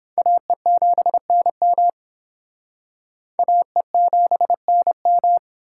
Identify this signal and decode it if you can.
Morse code, heard as a keyed tone, about 26 wpm, 700 Hz. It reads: AE7NM AE7NM
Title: Speed Racing - USA Call Signs 22wpm